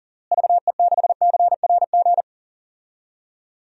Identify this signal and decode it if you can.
Morse code, heard as a keyed tone, about 40 wpm, 700 Hz. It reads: VE6CRG
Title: DXCC-100 Call Signs - 6 Characters - Repeated 3x - 40wpm